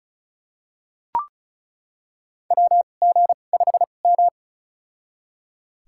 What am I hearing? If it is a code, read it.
Morse code, heard as a keyed tone, about 36 wpm, spaced as for 25 wpm, 700 Hz. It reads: WG5M
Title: USA Call Signs - 4 Characters - 2x Character Spacing - 35wpm